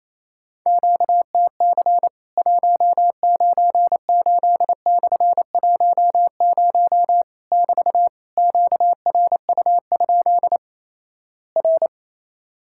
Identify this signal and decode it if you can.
Morse code, heard as a keyed tone, about 28 wpm, 700 Hz. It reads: QTC 198/10 <BT> QRU? R
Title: Worked All Europe DX Contest - QTC Exchange - 28wpm